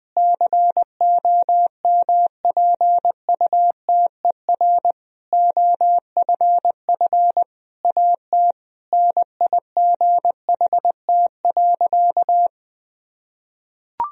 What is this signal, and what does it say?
Morse code, heard as a keyed tone, about 21 wpm, 700 Hz. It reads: COMPUTER OFF AT NIGHT.
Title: Sentences from Top 1000 Words - Review Part 3 - 20wpm